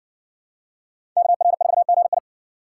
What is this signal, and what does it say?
Morse code, heard as a keyed tone, about 60 wpm, 700 Hz. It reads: KR4GI